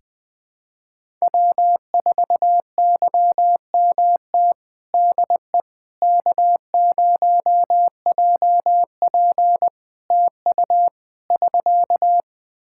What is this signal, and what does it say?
Morse code, heard as a keyed tone, about 20 wpm, 700 Hz. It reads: W4YMT DE K0JP TU <SK>
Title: Practice QSO - Exchange 4 - Ending - 20wpm